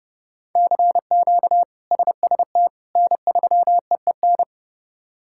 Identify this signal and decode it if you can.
Morse code, heard as a keyed tone, about 30 wpm, 700 Hz. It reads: CQ SST N3EEN
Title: K1USN SST QSO Practice - 30wpm